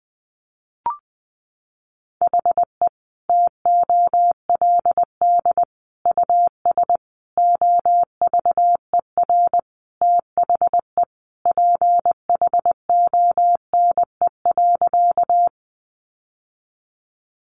Morse code, heard as a keyed tone, about 20 wpm, 700 Hz. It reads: HE TOLD US OVER THE PHONE.
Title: Sentences from Top 1100 Words 20wpm